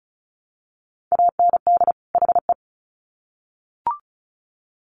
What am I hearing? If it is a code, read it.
Morse code, heard as a keyed tone, about 34 wpm, 700 Hz. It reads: AND HE
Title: Sets of 2 Words 35wpm